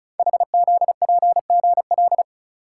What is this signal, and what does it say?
Morse code, heard as a keyed tone, about 36 wpm, 700 Hz. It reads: HZPGL
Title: Code Groups - 5 Letters - 35wpm